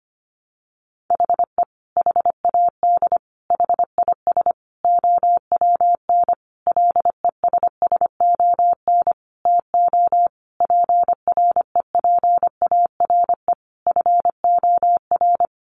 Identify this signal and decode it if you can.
Morse code, heard as a keyed tone, about 25 wpm, 700 Hz. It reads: HE HAD HIS OWN LESSON TO PREPARE FOR
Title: Sentences from Top 1200 Words 25wpm